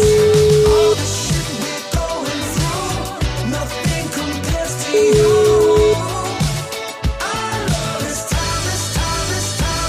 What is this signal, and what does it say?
Ringback tone (Europe, 425 Hz), with noise.